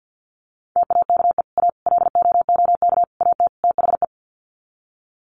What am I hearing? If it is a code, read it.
Morse code, heard as a keyed tone, about 50 wpm, 700 Hz. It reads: TAKE A LOOK AT THE
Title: Sets of 5 Words 50wpm